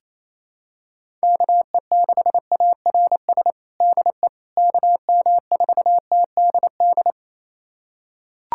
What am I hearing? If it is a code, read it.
Morse code, heard as a keyed tone, about 27 wpm, 700 Hz. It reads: KE6ARS DE KM4TDD
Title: USA Callsign DE Callsign - 6 Characters - 28wpm